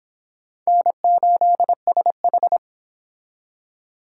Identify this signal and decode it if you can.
Morse code, heard as a keyed tone, about 25 wpm, 700 Hz. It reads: N8SH